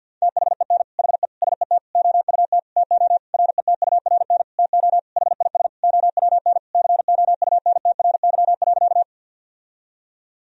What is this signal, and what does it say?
Morse code, heard as a keyed tone, about 52 wpm, 700 Hz. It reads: THEN HE SET OUT TO RETURN TO HIS OWN COUNTRY.
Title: Sentences from Top 1000 Words - Review Part 1 - 50wpm